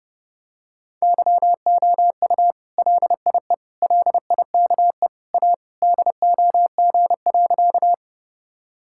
Morse code, heard as a keyed tone, about 32 wpm, 700 Hz. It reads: YOU LIE LIKE A DOG.
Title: Sentences from Top 1600 Words 30wpm